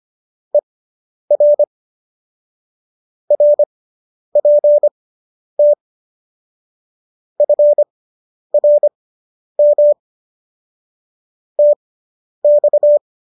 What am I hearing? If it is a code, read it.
Morse code, heard as a keyed tone, about 25 wpm, spaced as for 10 wpm, 590 Hz. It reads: ER RPT FRM TX